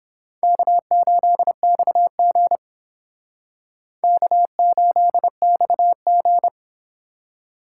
Morse code, heard as a keyed tone, about 30 wpm, 700 Hz. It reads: K8XG K8XG